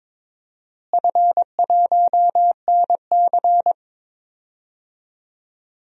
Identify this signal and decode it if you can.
Morse code, heard as a keyed tone, about 22 wpm, 700 Hz. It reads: F1NC
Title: DXCC-100 Call Signs - 4 Characters - Repeated 2x - 22wpm